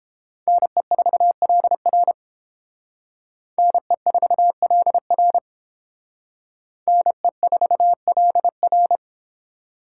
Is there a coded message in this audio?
Yes: Morse code from 0.5 to 9.0 s, heard as a keyed tone, about 29 wpm, 700 Hz. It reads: NE4LR NE4LR NE4LR